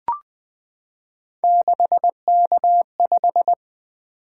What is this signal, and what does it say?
Morse code, heard as a keyed tone, about 20 wpm, 700 Hz. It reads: 6K5
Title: Three Letter-Number Combinations - 20wpm